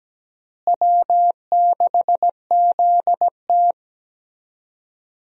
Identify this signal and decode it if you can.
Morse code, heard as a keyed tone, about 17 wpm, 700 Hz. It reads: W6ZT
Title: USA Call Signs - 4 Characters - Repeated 2x - 17wpm